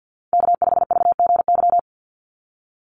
Morse code, heard as a keyed tone, about 50 wpm, 700 Hz. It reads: K5VGY
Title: USA Call Signs 50wpm